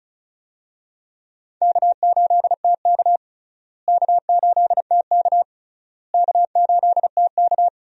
Morse code, heard as a keyed tone, about 36 wpm, 700 Hz. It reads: K8TK K8TK K8TK
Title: USA Call Signs - 4 Characters - Repeated 3x - 35wpm